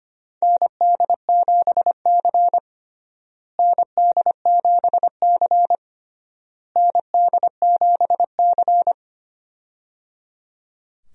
Morse code, heard as a keyed tone, about 25 wpm, 700 Hz. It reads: ND7C ND7C ND7C
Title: USA Call Signs - 4 Characters - Repeated 3x - 25wpm